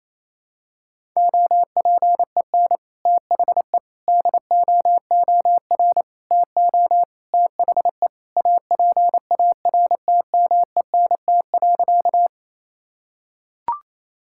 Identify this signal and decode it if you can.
Morse code, heard as a keyed tone, about 28 wpm, 700 Hz. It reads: OPEN THE DOOR TO THE APARTMENT.
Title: Sentences from Top 1000 Words - Review Part 6 - 28wpm